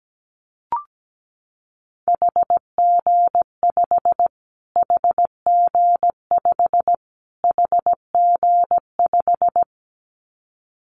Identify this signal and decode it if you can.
Morse code, heard as a keyed tone, about 17 wpm, 700 Hz. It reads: HG5 HG5 HG5
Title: Callsign Prefixes for Top 25 DXCC - Repeated 3x - 17wpm